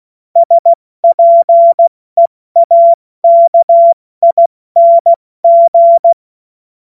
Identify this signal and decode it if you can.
Morse code, heard as a keyed tone, about 16 wpm, 680 Hz. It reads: SPEAKING
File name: Morsekurs / Witze / Witz 27. September 2022 70 F80